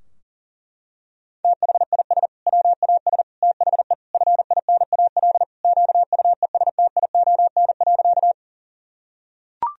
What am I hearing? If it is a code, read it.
Morse code, heard as a keyed tone, about 40 wpm, 700 Hz. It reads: THIS WAS THE FINAL QUESTION.
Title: Sentences from Top 1000 Words - Review Part 2 - 40wpm